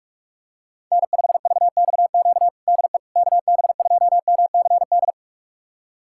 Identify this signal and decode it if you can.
Morse code, heard as a keyed tone, about 46 wpm, 700 Hz. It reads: N5VXQ DE KB2MCD
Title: USA Callsign DE Callsign 45wpm